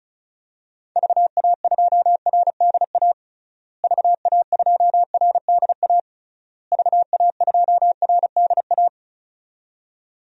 Morse code, heard as a keyed tone, about 36 wpm, 700 Hz. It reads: VA2RDA VA2RDA VA2RDA
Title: DXCC-100 Call Signs - 6 Characters - Repeated 3x - 35wpm